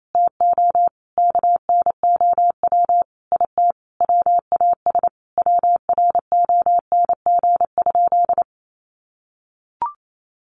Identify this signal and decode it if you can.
Morse code, heard as a keyed tone, about 28 wpm, 700 Hz. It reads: TO KNOW IT WAS WRONG?
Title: Sentences from Top 1100 Words 28wpm